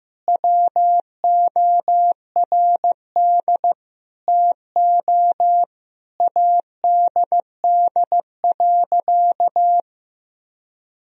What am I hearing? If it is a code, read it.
Morse code, heard as a keyed tone, about 16 wpm, 700 Hz. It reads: WORD TO ADD.
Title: Sentences from Top 1000 Words - Review Part 1 - 15wpm